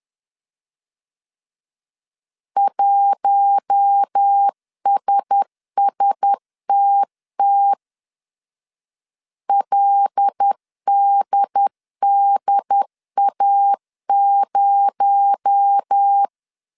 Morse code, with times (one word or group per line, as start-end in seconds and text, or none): 2.6-7.7 1SSTT
9.5-16.3 LDDA0